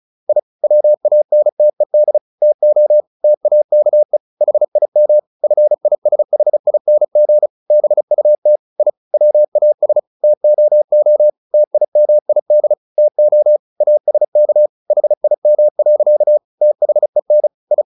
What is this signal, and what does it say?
Morse code, heard as a keyed tone, about 35 wpm, 600 Hz. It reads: I WANTED TO TAKE HIM FISHING BUT I WAS TOO TIMID TO ASK HIM. THEN I
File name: the old man and the sea / the old man and the sea 1 35wpm